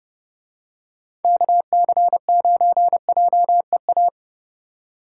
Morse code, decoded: KC9JEA

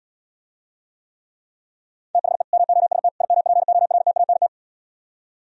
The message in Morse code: THE MOST IMPORTANT

